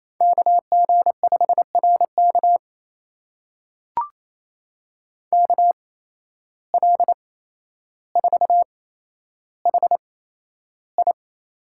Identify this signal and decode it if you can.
Morse code, heard as a keyed tone, about 28 wpm, spaced as for 12 wpm, 700 Hz. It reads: KG5RK KL4HI
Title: USA Call Signs - 5 Characters - 8x Character Spacing - 28wpm